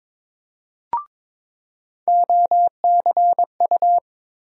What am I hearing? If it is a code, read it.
Morse code, heard as a keyed tone, about 22 wpm, 700 Hz. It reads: OCU
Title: USA Call Sign Postfixes - 3 Letters 22wpm